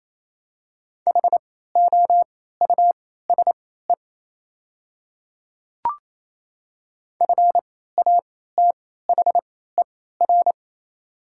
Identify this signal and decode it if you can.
Morse code, heard as a keyed tone, about 28 wpm, spaced as for 16 wpm, 700 Hz. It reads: HOUSE FATHER
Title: 3x Character Spacing - Top 500 Words - 28wpm